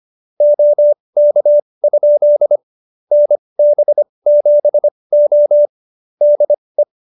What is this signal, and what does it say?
Morse code, heard as a keyed tone, about 25 wpm, spaced as for 20 wpm, 590 Hz. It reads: OK? NB7O DE